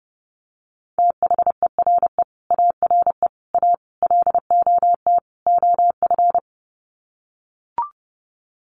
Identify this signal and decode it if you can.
Morse code, heard as a keyed tone, about 28 wpm, 700 Hz. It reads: THERE ARE A LOT OF
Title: Sets of 5 Words 30wpm